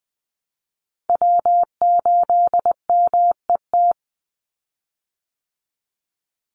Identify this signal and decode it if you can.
Morse code, heard as a keyed tone, about 20 wpm, 700 Hz. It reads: W8MET
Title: Call Signs Repeated 2x - 20wpm